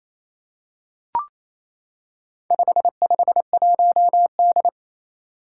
Morse code, heard as a keyed tone, about 28 wpm, 700 Hz. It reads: 551D
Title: Four Challenging Letter-Number Combinations 28wpm